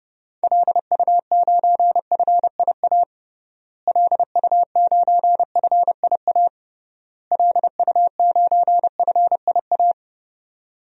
Morse code, heard as a keyed tone, about 30 wpm, 700 Hz. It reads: LU9FIA LU9FIA LU9FIA
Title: DXCC-100 Call Signs - 6 Characters - Repeated 3x - 30wpm